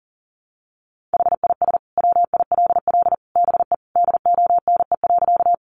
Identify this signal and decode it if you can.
Morse code, heard as a keyed tone, about 40 wpm, 700 Hz. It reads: HIS WILL BE DONE.
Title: Sentences from Top 500 Words 40wpm